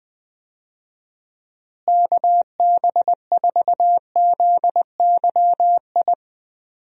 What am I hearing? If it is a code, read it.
Morse code, heard as a keyed tone, about 20 wpm, 700 Hz. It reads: KB4ZYI
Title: USA Call Signs - 6 Characters - Repeated 2x - 20wpm